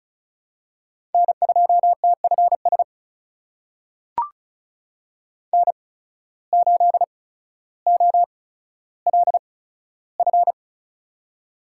Morse code, heard as a keyed tone, about 36 wpm, spaced as for 15 wpm, 700 Hz. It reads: N2TFS N8OLF
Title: USA Call Signs - 8x Character Spacing - 35wpm